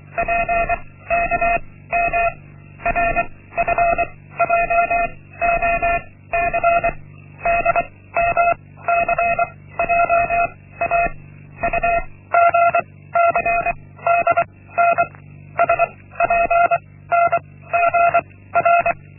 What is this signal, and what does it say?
Morse code, heard as a keyed tone, about 23 wpm, 690 Hz. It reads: P K M R F J O C D M C J A U G C D N S P N G R
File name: Linksmakalnis-1